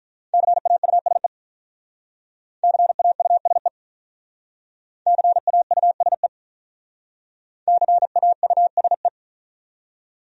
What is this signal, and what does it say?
Morse code, heard as a keyed tone, about 43 wpm, 700 Hz. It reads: CAUSE CAUSE CAUSE CAUSE
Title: Speed-Racing Top 500 Words 35wpm